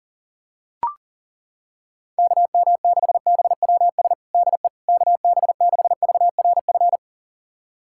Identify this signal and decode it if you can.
Morse code, heard as a keyed tone, about 40 wpm, 700 Hz. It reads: KM6BWS DE KB6VRF
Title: USA Callsign DE Callsign - 6 Characters - 40wpm